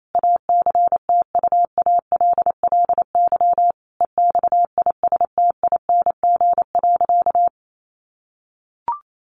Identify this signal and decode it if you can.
Morse code, heard as a keyed tone, about 28 wpm, 700 Hz. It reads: ACTUALLY EXISTING.
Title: Sentences from Top 1500 Words 28wpm